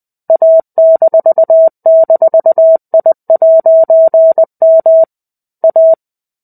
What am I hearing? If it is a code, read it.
Morse code, heard as a keyed tone, about 20 wpm, 650 Hz. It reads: A--I'M A
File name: 05-Chapter-05 20 WPM